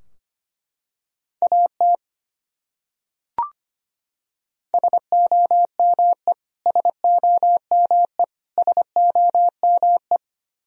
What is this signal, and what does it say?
Morse code, heard as a keyed tone, about 25 wpm, 700 Hz. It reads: AT SOME SOME SOME
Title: Top 100 Words Repeated 25wpm